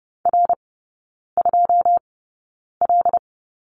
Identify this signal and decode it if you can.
Morse code, heard as a keyed tone, about 30 wpm, spaced as for 9 wpm, 700 Hz. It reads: R2L
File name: Lesson 18e - Letter m - Three Characters 30wpm